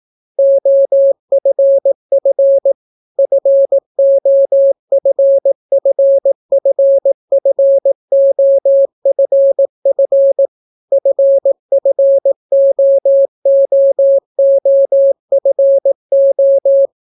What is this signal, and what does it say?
Morse code, heard as a keyed tone, about 18 wpm, 550 Hz. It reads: OFF FOFFFFOFF FFOOOFO